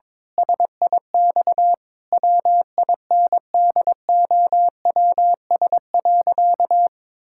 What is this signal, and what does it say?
Morse code, heard as a keyed tone, about 22 wpm, 700 Hz. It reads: SIX WINDOWS.
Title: Pangram Challenge v2 - 22wpm